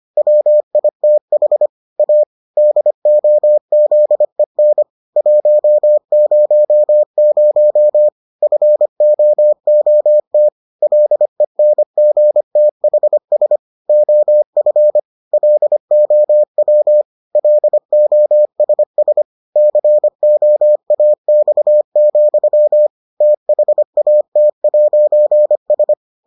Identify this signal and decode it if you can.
Morse code, heard as a keyed tone, about 25 wpm, 600 Hz. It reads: WITH A DOZEN 100 FOOT LENGTHS OF LOW LOSS COAX, THAT'S